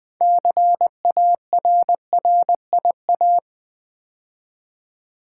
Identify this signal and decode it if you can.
Morse code, heard as a keyed tone, about 20 wpm, 700 Hz. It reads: CARRIA
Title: Build A Word - 3 Letters At A Time - 8-Letter Words - 20wpm